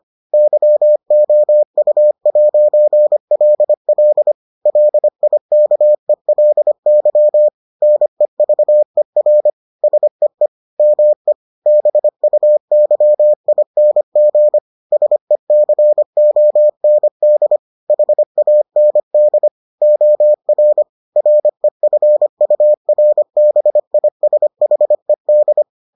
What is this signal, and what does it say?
Morse code, heard as a keyed tone, about 25 wpm, 600 Hz. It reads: YOU'LL LIKELY NEVER SEE ME BUYING SECOND HAND OR REFURBISHED